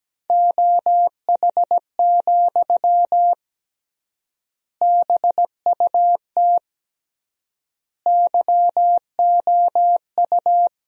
Morse code, heard as a keyed tone, about 17 wpm, 700 Hz. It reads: OH, BUT YOU
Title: 3x Word-Spacing - Sentences from Top 500 Words - 17wpm